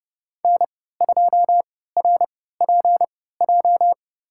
Morse code, heard as a keyed tone, about 30 wpm, spaced as for 17 wpm, 700 Hz. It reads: N2RPJ
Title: USA Call Signs - 5 Characters - 3x Character Spacing - 30wpm